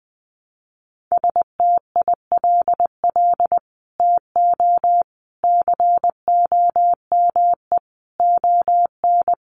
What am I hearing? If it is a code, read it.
Morse code, heard as a keyed tone, about 20 wpm, 700 Hz. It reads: STILL TO COME ON
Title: Sets of 5 Words Encore2 20wpm